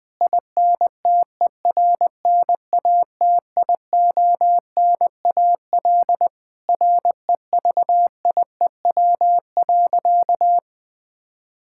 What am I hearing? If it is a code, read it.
Morse code, heard as a keyed tone, about 21 wpm, 700 Hz. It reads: INTERNATIONAL REVIEW.
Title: Sentences from Top 1000 Words - Review Part 6 - 20wpm